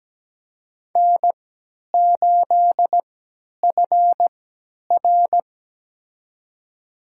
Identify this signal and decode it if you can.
Morse code, heard as a keyed tone, about 17 wpm, spaced as for 10 wpm, 700 Hz. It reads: N8FR
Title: USA Call Signs - 4 Characters - 3x Character Spacing - 17wpm